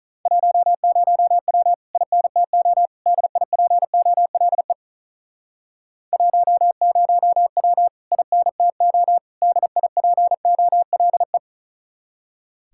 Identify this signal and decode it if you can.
Morse code, heard as a keyed tone, about 42 wpm, 700 Hz. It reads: 10W INTO DIPOLE 10W INTO DIPOLE
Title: Speed-Racing Common QSO Phrases - 35wpm